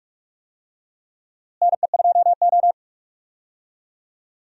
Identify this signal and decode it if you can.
Morse code, heard as a keyed tone, about 46 wpm, 700 Hz. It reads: NE2O
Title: USA Call Signs - 4 Characters - Repeated 3x - 45wpm